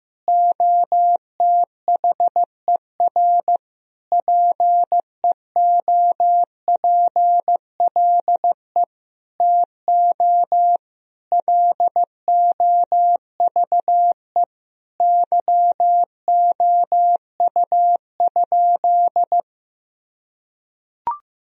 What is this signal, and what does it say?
Morse code, heard as a keyed tone, about 15 wpm, 700 Hz. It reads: OTHER PEOPLE TO LOVE YOU?
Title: Sentences from Top 1000 Words - Review Part 3 - 15wpm